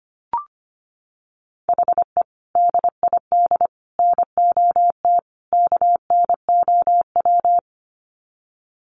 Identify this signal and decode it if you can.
Morse code, heard as a keyed tone, about 25 wpm, 700 Hz. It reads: HE DID NOT KNOW
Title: Sets of 4 Words 25wpm